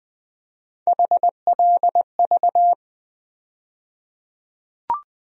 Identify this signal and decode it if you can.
Morse code, heard as a keyed tone, about 20 wpm, 700 Hz. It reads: HLV
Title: USA Call Sign Postfixes - 3 Letters 20wpm